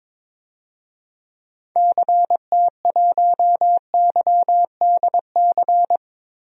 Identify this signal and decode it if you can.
Morse code, heard as a keyed tone, about 21 wpm, 700 Hz. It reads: CT1YDC
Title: DXCC-100 Call Signs - 6 Characters - Repeated 3x - 22wpm